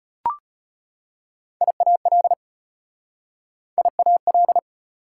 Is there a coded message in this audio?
Yes: Morse code from 1.6 to 4.6 s, heard as a keyed tone, about 33 wpm, 700 Hz. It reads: IAL IAL